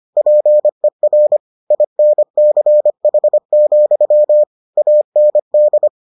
Morse code, heard as a keyed tone, about 25 wpm, 600 Hz. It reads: PER INCH, AND